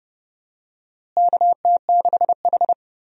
Morse code, heard as a keyed tone, about 29 wpm, 700 Hz. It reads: KT6H
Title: USA Call Signs - 4 Characters - Repeated 2x - 30wpm